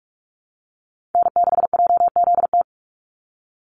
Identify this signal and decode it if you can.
Morse code, heard as a keyed tone, about 46 wpm, 690 Hz. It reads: N6JZT
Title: NA SOTA Chaser Callsigns 45wpm